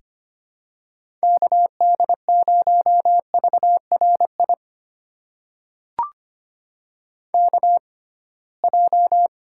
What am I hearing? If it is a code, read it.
Morse code, heard as a keyed tone, about 25 wpm, spaced as for 19 wpm, 700 Hz. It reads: KD0VRI KJ
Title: USA Call Signs - 6 Characters - 6x Character Spacing - 25wpm